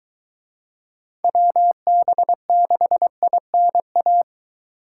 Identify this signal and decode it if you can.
Morse code, heard as a keyed tone, about 22 wpm, 700 Hz. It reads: WB6INA